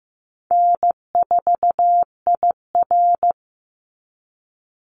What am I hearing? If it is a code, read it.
Morse code, heard as a keyed tone, about 15 wpm, 700 Hz. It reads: N4IR
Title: Easy Call Signs Repeated 2x - 15wpm